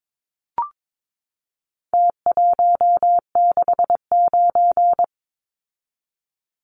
Morse code, heard as a keyed tone, about 22 wpm, 700 Hz. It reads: T169
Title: Numbers - Contest Style - 4-Digit 22wpm